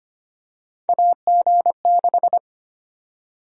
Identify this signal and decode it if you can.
Morse code, heard as a keyed tone, about 25 wpm, 700 Hz. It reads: AG6
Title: USA Call Sign Prefixes - 25wpm